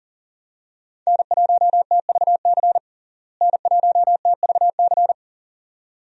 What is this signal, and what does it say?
Morse code, heard as a keyed tone, about 40 wpm, 700 Hz. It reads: N1TVC N1TVC
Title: USA Call Signs - 5 Characters - Repeated 2x - 40wpm